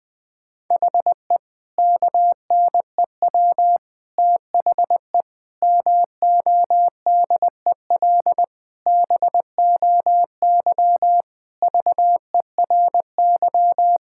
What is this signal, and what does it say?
Morse code, heard as a keyed tone, about 20 wpm, 700 Hz. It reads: HE KNEW THE MODEL BOY VERY
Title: Sentences from Top 1000 Words - Review Part 5 - 20wpm